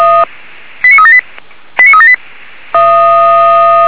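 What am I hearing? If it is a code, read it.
Five-tone selective call (ZVEI), heard as quick runs of five tones, tones of about 70 ms. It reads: 89378 89378
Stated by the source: ZVEI five-tone selective call 89373 received on an SDR.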